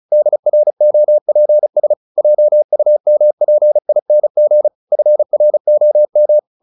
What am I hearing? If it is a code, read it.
Morse code, heard as a keyed tone, about 37 wpm, 600 Hz. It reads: DROPS JUMPING FROM
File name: the old man and the sea 2 35wpm